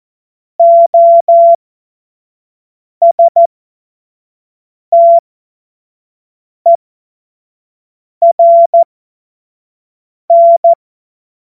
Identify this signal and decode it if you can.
Morse code, heard as a keyed tone, about 14 wpm, spaced as for 5 wpm, 680 Hz. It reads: OSTERN
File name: Morsekurs / Witze / Witz 07. Februar 2023 25 F70